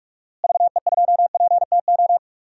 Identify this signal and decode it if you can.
Morse code, heard as a keyed tone, about 44 wpm, 700 Hz. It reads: VE2PTO
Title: DXCC-100 Call Signs - 6 Characters - Repeated 3x - 45wpm